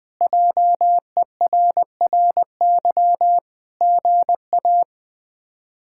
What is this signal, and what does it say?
Morse code, heard as a keyed tone, about 20 wpm, 700 Hz. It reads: JERRY GA
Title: K1USN SST QSO Practice - 20wpm